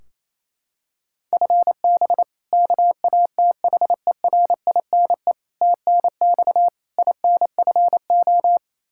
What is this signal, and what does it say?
Morse code, heard as a keyed tone, about 28 wpm, 700 Hz. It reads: FB KATHERINE TNX INFO
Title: Practice QSO - Exchange 3 - More Info - 28wpm